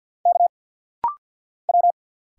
Morse code, heard as a keyed tone, about 50 wpm, 690 Hz. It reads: K W
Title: Single Letters - Rapid-Fire 50wpm